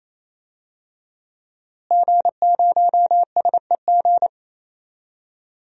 Morse code, heard as a keyed tone, about 28 wpm, 700 Hz. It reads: G0SEG
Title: DXCC-100 Call Signs - 5 Characters - Repeated 3x - 28wpm